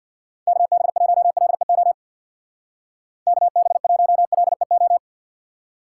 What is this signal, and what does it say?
Morse code, heard as a keyed tone, about 65 wpm, 700 Hz. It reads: KB1LEO KB1LEO